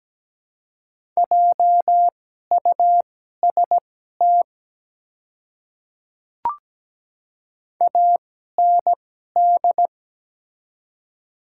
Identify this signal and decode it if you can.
Morse code, heard as a keyed tone, about 17 wpm, spaced as for 12 wpm, 700 Hz. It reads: JUST AND
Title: 2x Character Spacing - Top 200 Words - 17wpm